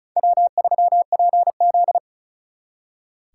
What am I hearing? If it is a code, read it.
Morse code, heard as a keyed tone, about 43 wpm, 700 Hz. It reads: W3PZ
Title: Speed Racing - USA 1x2 Call Signs 35wpm